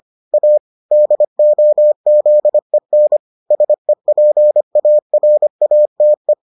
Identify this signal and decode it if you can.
Morse code, heard as a keyed tone, about 25 wpm, 600 Hz. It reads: A DOZEN SEPARATE